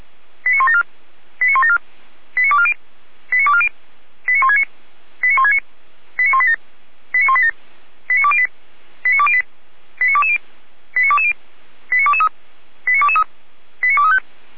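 Five-tone selective call (ZVEI), heard as quick runs of five tones, tones of about 80 ms. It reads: 89174 89174 89240 89240 89169 89169 89187 89187 89208 89208 89220 89220 89223 89223 89235
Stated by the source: ZVEI five-tone selective call received on SDR.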